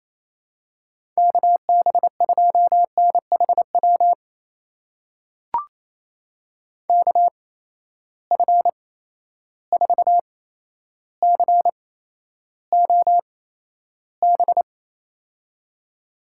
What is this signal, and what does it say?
Morse code, heard as a keyed tone, about 28 wpm, spaced as for 12 wpm, 700 Hz. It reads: KB2NHW KF4COB
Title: USA Call Signs - 6 Characters - 8x Character Spacing - 28wpm